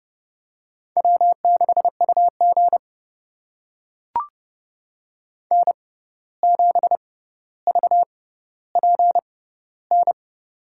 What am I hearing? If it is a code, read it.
Morse code, heard as a keyed tone, about 32 wpm, spaced as for 15 wpm, 700 Hz. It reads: W6UG N7VPN
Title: USA Call Signs - 6x Character Spacing - 30wpm